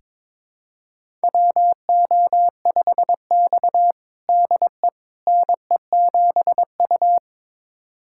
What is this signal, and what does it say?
Morse code, heard as a keyed tone, about 22 wpm, 700 Hz. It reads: WO5X DE NE7U
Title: USA Callsign DE Callsign - 4 Characters - 22wpm